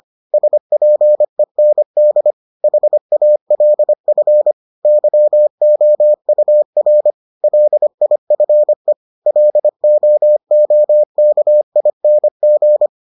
Morse code, heard as a keyed tone, about 25 wpm, 600 Hz. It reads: SPEND HALF YOUR LIFE LOOKING